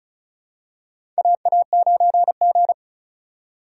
Morse code, heard as a keyed tone, about 36 wpm, 700 Hz. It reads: AA9G
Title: USA Call Signs - 2x Character Spacing - 35wpm